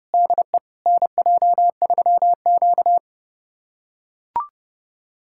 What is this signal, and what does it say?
Morse code, heard as a keyed tone, about 32 wpm, 700 Hz. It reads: DE NJ3Q
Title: USA Callsign DE Callsign - 4 Characters - 30wpm